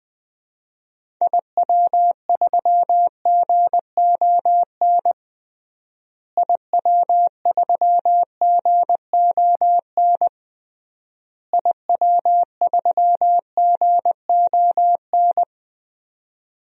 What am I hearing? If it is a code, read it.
Morse code, heard as a keyed tone, about 20 wpm, 700 Hz. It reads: IW3GON IW3GON IW3GON
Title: DXCC-100 Call Signs - 6 Characters - Repeated 3x - 20wpm